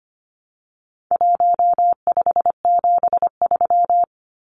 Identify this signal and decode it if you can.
Morse code, heard as a keyed tone, about 25 wpm, 700 Hz. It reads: 1573
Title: Numbers - Contest Style - 4-Digit 25wpm